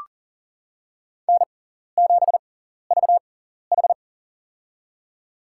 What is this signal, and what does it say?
Morse code, heard as a keyed tone, about 42 wpm, spaced as for 14 wpm, 690 Hz. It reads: N7VH
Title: USA Call Signs - 4 Characters - 6x Character Spacing - 40wpm